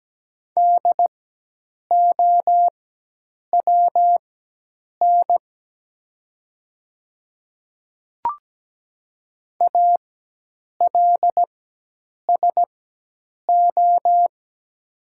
Morse code, heard as a keyed tone, about 17 wpm, spaced as for 8 wpm, 700 Hz. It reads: DOWN ALSO
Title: 4x Character Spacing - Top 200 Words - 17wpm